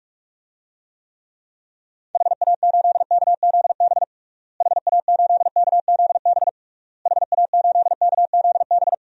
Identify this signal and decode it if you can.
Morse code, heard as a keyed tone, about 42 wpm, 700 Hz. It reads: HA8KZB HA8KZB HA8KZB
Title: DXCC-100 Call Signs - 6 Characters - Repeated 3x - 45wpm